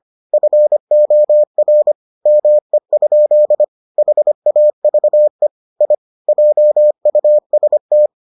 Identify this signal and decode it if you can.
Morse code, heard as a keyed tone, about 25 wpm, 600 Hz. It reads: FOR ME? HAVE I JUST